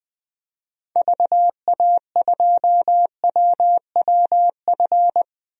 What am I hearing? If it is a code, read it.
Morse code, heard as a keyed tone, about 21 wpm, 700 Hz. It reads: VA2WWF
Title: DXCC-100 Call Signs - 6 Characters - 20wpm